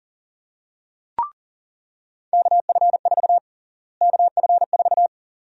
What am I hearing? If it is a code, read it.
Morse code, heard as a keyed tone, about 40 wpm, 700 Hz. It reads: KF4 KF4
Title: USA Call Sign Prefixes - Repeated 2x - 40wpm